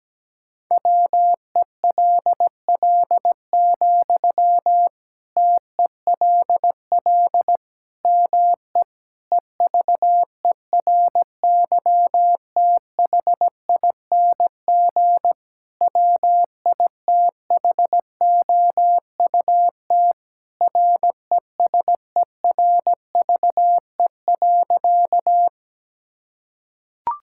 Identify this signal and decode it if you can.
Morse code, heard as a keyed tone, about 17 wpm, 700 Hz. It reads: WELL, TELL ME EVERYTHING WITHOUT RESERVE.
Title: Sentences from Top 1000 Words - Review Part 6 - 17wpm